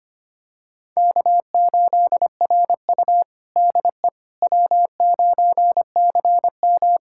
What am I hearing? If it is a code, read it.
Morse code, heard as a keyed tone, about 25 wpm, 700 Hz. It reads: K8RU DE W9CM
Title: USA Callsign DE Callsign - 4 Characters - 25wpm